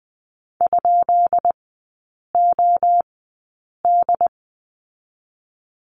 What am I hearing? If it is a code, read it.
Morse code, heard as a keyed tone, about 20 wpm, 690 Hz. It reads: ? O D
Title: 145 Lesson 24e - Letter y - Three Characters 20wpm